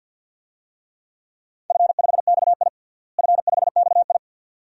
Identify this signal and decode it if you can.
Morse code, heard as a keyed tone, about 46 wpm, 700 Hz. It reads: F5XI F5XI
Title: DXCC-100 Call Signs - 4 Characters - Repeated 2x - 50wpm